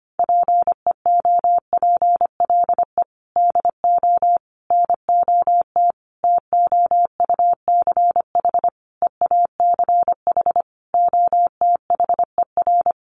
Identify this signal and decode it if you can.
Morse code, heard as a keyed tone, about 25 wpm, 700 Hz. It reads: PEOPLE DO NOT TOUCH EACH OTHER
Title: Sentences from Top 1200 Words 25wpm